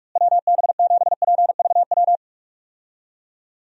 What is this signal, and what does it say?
Morse code, heard as a keyed tone, about 46 wpm, 700 Hz. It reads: WB7PVW
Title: USA Call Signs - 6 Characters - Repeated 2x - 45wpm